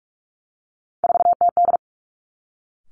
Morse code, heard as a keyed tone, about 46 wpm, 700 Hz. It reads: 4TD
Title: Three Challenging Letter-Number Combinations 45wpm